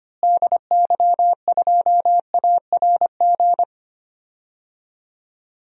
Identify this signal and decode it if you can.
Morse code, heard as a keyed tone, about 25 wpm, 700 Hz. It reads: DY2ARG
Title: DXCC-100 Call Signs - 6 Characters - Repeated 3x - 25wpm